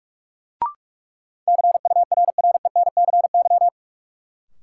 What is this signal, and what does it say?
Morse code, heard as a keyed tone, about 46 wpm, 700 Hz. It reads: CURRENCY
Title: Sentences from Top 1500 Words - New Word Drill - 45wpm